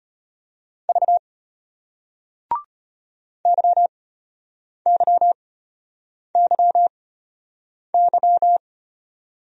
Morse code, heard as a keyed tone, about 34 wpm, spaced as for 8 wpm, 700 Hz. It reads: V YYYY